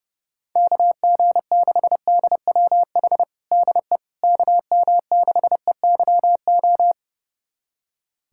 Morse code, heard as a keyed tone, about 32 wpm, 700 Hz. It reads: KG6DWH DE KM6EYO
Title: USA Callsign DE Callsign - 6 Characters - 30wpm